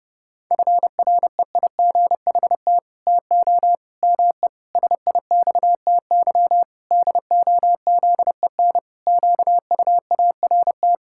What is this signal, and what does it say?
Morse code, heard as a keyed tone, about 32 wpm, 700 Hz. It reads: FREIGHT TO ME SIXTY DOZEN QUART
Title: Pangram Challenge v2 - 30wpm